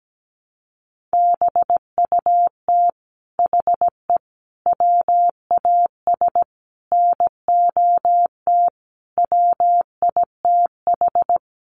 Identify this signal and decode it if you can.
Morse code, heard as a keyed tone, about 17 wpm, 700 Hz. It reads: BUT HE WAS NOT WITH
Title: Sentences from Top 1500 Words 17wpm